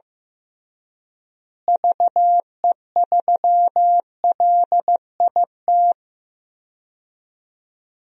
Morse code, heard as a keyed tone, about 15 wpm, 700 Hz. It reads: VE3LIT
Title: DXCC-100 Call Signs - 6 Characters - Repeated 3x - 15wpm